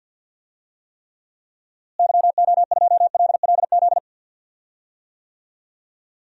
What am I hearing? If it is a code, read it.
Morse code, heard as a keyed tone, about 50 wpm, 700 Hz. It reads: YO2LLZ
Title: DXCC-100 Call Signs - 6 Characters - Repeated 2x - 50wpm